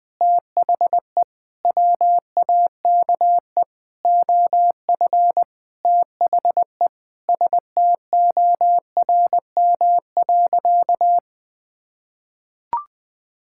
Morse code, heard as a keyed tone, about 20 wpm, 700 Hz. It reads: THE WAKE OF THE STORM.
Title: Sentences with 4-Word Phrases - Spoken Beginning - 20wpm